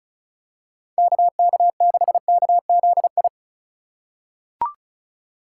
Morse code, heard as a keyed tone, about 36 wpm, 700 Hz. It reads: KK6KZI
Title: USA Call Signs - 6 Characters - 3x Character Spacing - 35wpm